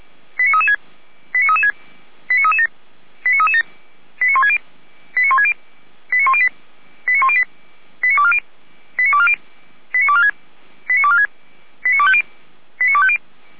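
Five-tone selective call (ZVEI), heard as quick runs of five tones, tones of about 70 ms. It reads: 89336 89336 89337 89337 89160 89160 89108 89108 89240 89240 89246 89246 89250 89250
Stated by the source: ZVEI five-tone selective call received on SDR.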